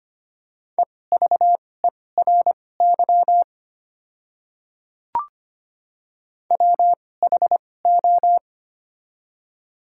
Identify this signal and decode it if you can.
Morse code, heard as a keyed tone, about 25 wpm, spaced as for 18 wpm, 700 Hz. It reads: EVERY WHO
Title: 2x Character Spacing - Top 200 Words - 25wpm